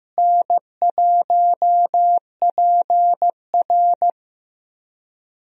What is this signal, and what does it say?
Morse code, heard as a keyed tone, about 15 wpm, 700 Hz. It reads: N1PR
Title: USA Call Signs - 4 Characters - Repeated 2x - 15wpm